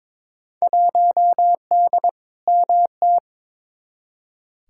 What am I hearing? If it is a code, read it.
Morse code, heard as a keyed tone, about 22 wpm, 700 Hz. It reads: 1D MT
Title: Field Day Exchanges - 22wpm